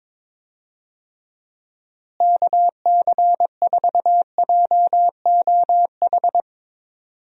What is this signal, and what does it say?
Morse code, heard as a keyed tone, about 22 wpm, 700 Hz. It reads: KC4JOH
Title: USA Call Signs - 6 Characters - Repeated 2x - 22wpm